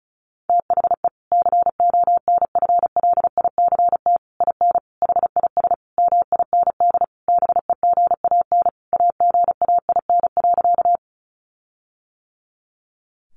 Morse code, heard as a keyed tone, about 34 wpm, 700 Hz. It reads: THE CONFLICT IN HIS MIND BEGAN AGAIN.
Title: Sentences from Top 1300 Words 35wpm